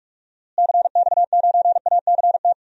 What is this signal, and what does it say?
Morse code, heard as a keyed tone, about 43 wpm, 700 Hz. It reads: CX9ACT
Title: DXCC-100 Call Signs - 6 Characters - Repeated 2x - 45wpm